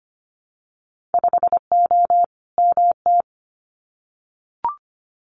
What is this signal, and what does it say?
Morse code, heard as a keyed tone, about 25 wpm, 700 Hz. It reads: 5O MT
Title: Winter Field Day Exchanges Encore 25wpm